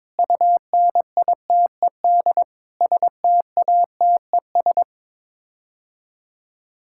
Morse code, heard as a keyed tone, about 22 wpm, 700 Hz. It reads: UNITED STATES
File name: practice / 22wpm / Sets of 7 Words - 22wpm